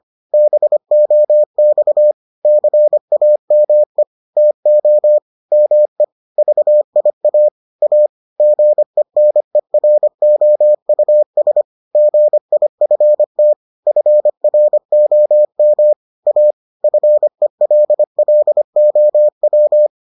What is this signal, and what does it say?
Morse code, heard as a keyed tone, about 25 wpm, 600 Hz. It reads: BOX CAME TO ME VIA A GENEROUS GIFT FROM A FELLOW